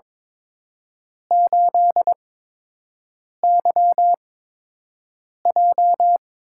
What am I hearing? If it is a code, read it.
Morse code, heard as a keyed tone, about 22 wpm, spaced as for 6 wpm, 700 Hz. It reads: 8YJ
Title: USA Call Signs - 4 Characters - 8x Character Spacing - 22wpm